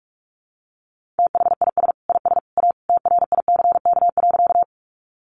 Morse code, heard as a keyed tone, about 44 wpm, 700 Hz. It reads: THIS IS A TRICK.
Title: Sentences from Top 1500 Words 45wpm